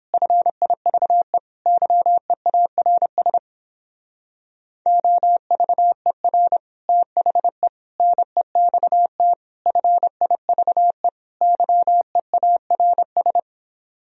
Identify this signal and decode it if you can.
Morse code, heard as a keyed tone, about 29 wpm, 700 Hz. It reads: FIVE YEARS OVER THE NEXT FIVE YEARS